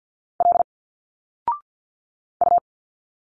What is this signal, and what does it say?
Morse code, heard as a keyed tone, about 49 wpm, 700 Hz. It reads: L U
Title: Single Letters 50wpm